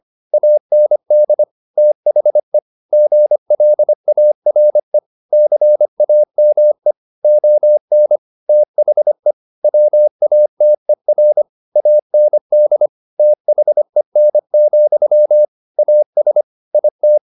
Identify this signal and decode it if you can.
Morse code, heard as a keyed tone, about 25 wpm, 600 Hz. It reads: AND THE GLARE CAME ON THE WATER AND THEN, AS IT
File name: the old man and the sea part 1 25wpm